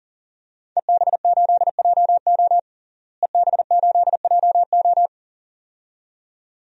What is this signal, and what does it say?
Morse code, heard as a keyed tone, about 40 wpm, 700 Hz. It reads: EB8JO EB8JO
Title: DXCC-100 Call Signs - 5 Characters - Repeated 2x - 40wpm